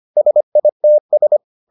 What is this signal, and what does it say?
Morse code, heard as a keyed tone, about 25 wpm, 600 Hz. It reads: SITS